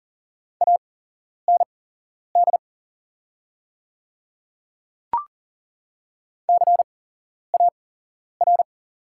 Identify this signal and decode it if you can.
Morse code, heard as a keyed tone, about 40 wpm, spaced as for 11 wpm, 700 Hz. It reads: AND CAR
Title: 8x Character Spacing - Top 500 Words - 40wpm